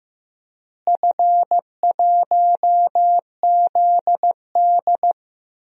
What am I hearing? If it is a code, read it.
Morse code, heard as a keyed tone, about 15 wpm, 700 Hz. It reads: F1ZD
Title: DXCC-100 Call Signs - 4 Characters - Repeated 2x - 15wpm